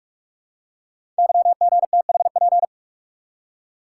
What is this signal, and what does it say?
Morse code, heard as a keyed tone, about 44 wpm, 690 Hz. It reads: YGTHP
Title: Code Groups - 5 Letters - 45wpm